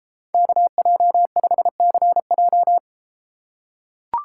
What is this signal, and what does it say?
Morse code, heard as a keyed tone, about 34 wpm, 700 Hz. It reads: KJ5CJ